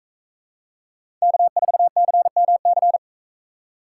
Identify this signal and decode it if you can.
Morse code, heard as a keyed tone, about 41 wpm, 700 Hz. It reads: K4CMC